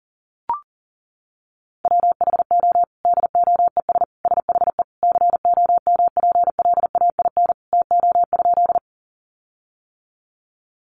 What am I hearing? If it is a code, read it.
Morse code, heard as a keyed tone, about 40 wpm, 700 Hz. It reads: WHO DOES SHE COMPLAIN TO?
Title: Sentences from Top 1300 Words 40wpm